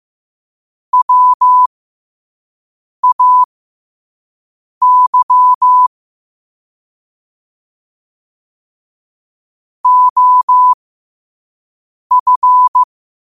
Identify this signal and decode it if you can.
Morse code, heard as a keyed tone, about 15 wpm, spaced as for 5 wpm, 1000 Hz. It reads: WAY OF
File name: qotdfiles / 5wpm